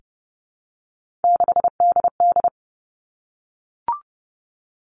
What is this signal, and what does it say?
Morse code, heard as a keyed tone, about 30 wpm, 700 Hz. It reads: 6DD
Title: Three Challenging Letter-Number Combinations 30wpm